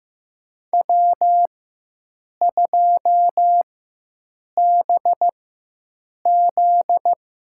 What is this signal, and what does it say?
Morse code, heard as a keyed tone, about 15 wpm, spaced as for 7 wpm, 700 Hz. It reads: W2BZ
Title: USA Call Signs - 4x Character Spacing - 15wpm